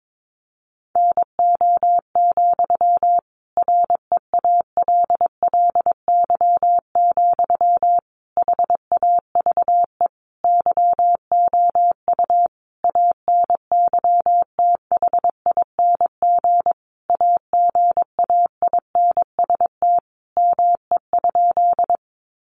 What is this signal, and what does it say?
Morse code, heard as a keyed tone, about 22 wpm, 700 Hz. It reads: NO, REALLY, HAVE YOU ANYTHING AGAINST ME?
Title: Sentences from Top 1000 Words 22wpm